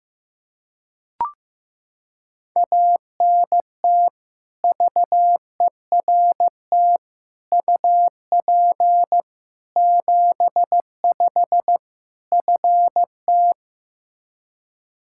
Morse code, heard as a keyed tone, about 16 wpm, 700 Hz. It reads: ANT VERT UP 75 FT
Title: K1USN SST QSO Practice 15wpm